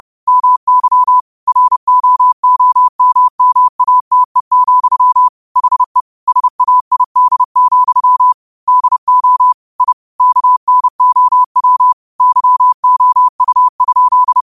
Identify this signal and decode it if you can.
Morse code, heard as a keyed tone, about 30 wpm, 1000 Hz. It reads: MY ROOMMATE, HE SAID, DO I KNOW YOU?